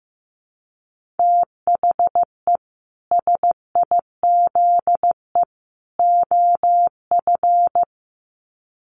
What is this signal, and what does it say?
Morse code, heard as a keyed tone, about 15 wpm, 700 Hz. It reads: THE SIZE OF
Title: Sets of 3 Words 15wpm